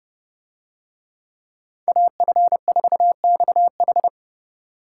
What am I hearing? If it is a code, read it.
Morse code, heard as a keyed tone, about 29 wpm, 700 Hz. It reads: AF4XH